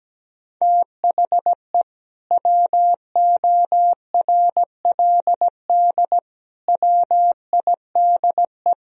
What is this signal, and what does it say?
Morse code, heard as a keyed tone, about 17 wpm, 700 Hz. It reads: THE WORLD WIDE
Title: Sentences from Top 1000 Words - Review Part 4 - 17wpm